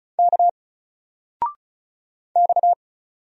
Morse code, heard as a keyed tone, about 36 wpm, 700 Hz. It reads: K X
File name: Single-Letter-Number-35wpm